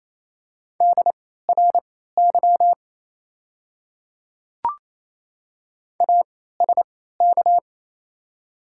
Morse code, heard as a keyed tone, about 27 wpm, spaced as for 16 wpm, 690 Hz. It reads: DRY ASK
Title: 3x Character Spacing - Top 500 Words - 28wpm